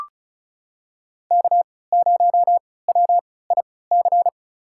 Morse code, heard as a keyed tone, about 36 wpm, spaced as for 20 wpm, 700 Hz. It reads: K0WIC
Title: USA Call Signs - 5 Characters - 3x Character Spacing - 35wpm